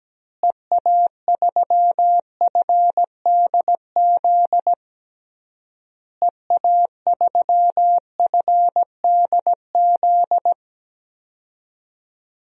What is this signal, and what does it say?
Morse code, heard as a keyed tone, about 17 wpm, 700 Hz. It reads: EA3FDZ EA3FDZ
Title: DXCC-100 Call Signs - 6 Characters - Repeated 3x - 17wpm